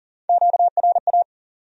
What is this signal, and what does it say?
Morse code, heard as a keyed tone, about 55 wpm, spaced as for 47 wpm, 700 Hz. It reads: QRA
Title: Speed-Racing Q-Codes - 40wpm